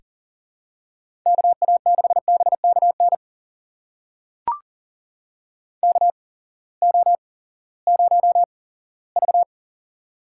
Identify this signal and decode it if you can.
Morse code, heard as a keyed tone, about 40 wpm, spaced as for 20 wpm, 700 Hz. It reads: KA6BKN KO0V
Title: USA Call Signs - 8x Character Spacing - 40wpm